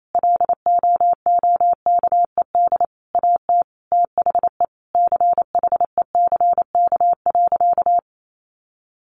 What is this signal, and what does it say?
Morse code, heard as a keyed tone, about 28 wpm, 700 Hz. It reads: LOOKED AT THE CHECK.
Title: A Sound Of Thunder 28wpm